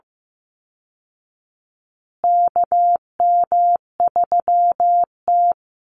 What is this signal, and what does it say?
Morse code, heard as a keyed tone, about 15 wpm, 700 Hz. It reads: KM3T
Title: Call Signs Repeated 2x - 15wpm